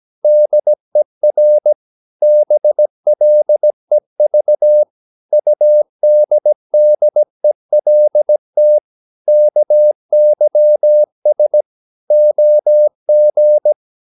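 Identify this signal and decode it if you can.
Morse code, heard as a keyed tone, about 17 wpm, 600 Hz. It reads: DER BLEV UDDELT KYS OG
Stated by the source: Slow (QRS) Morse practice at 17 wpm, from CW med Gnister.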